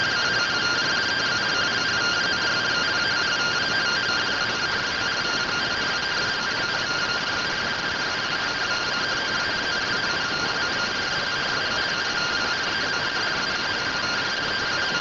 Two-tone teleprinter signal, about 170 Hz shift.